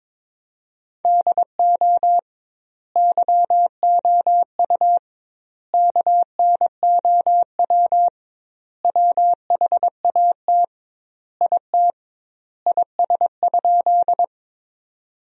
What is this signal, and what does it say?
Morse code, heard as a keyed tone, about 22 wpm, 700 Hz. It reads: DO YOU KNOW WHAT IT IS?
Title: 2x Word-Spacing - Sentences from Top 100 Words - 22wpm